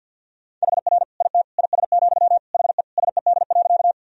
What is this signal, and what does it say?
Morse code, heard as a keyed tone, about 50 wpm, 700 Hz. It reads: HR IT IS, HE SED.
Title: Sentences from Top 200 Words and Abbreviations 50wpm